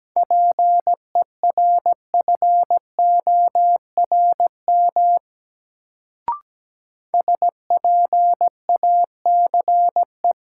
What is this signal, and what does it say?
Morse code, heard as a keyed tone, about 17 wpm, 700 Hz. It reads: PERFORM SPACE
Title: Sentences from Top 1000 Words - Review Part 5 - New Word Drill - 17wpm